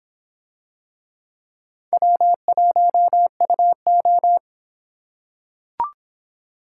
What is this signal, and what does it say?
Morse code, heard as a keyed tone, about 26 wpm, 700 Hz. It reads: W1UO